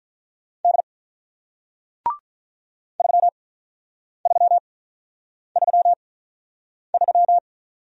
Morse code, heard as a keyed tone, about 47 wpm, spaced as for 9 wpm, 700 Hz. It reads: D 3333